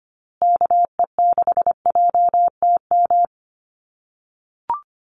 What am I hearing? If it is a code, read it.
Morse code, heard as a keyed tone, about 25 wpm, 700 Hz. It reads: KE6JTM
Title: USA Call Signs Repeated 3x 25wpm